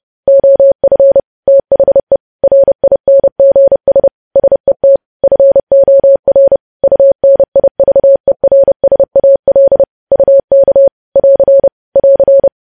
Morse code, heard as a keyed tone, about 30 wpm, 550 Hz. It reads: OF THE RINGS SET FOR UNIVERSAL UK <AR> <AR>
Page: news headlines in morse code at 30 wpm